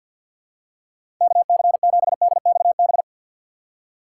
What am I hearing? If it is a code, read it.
Morse code, heard as a keyed tone, about 47 wpm, 700 Hz. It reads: KC7DXB
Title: USA Call Signs - 6 Characters - 4x Character Spacing - 50wpm